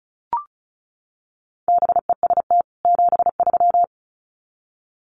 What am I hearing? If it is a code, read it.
Morse code, heard as a keyed tone, about 34 wpm, 700 Hz. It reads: BEST 73
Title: Common QSO Phrases 35wpm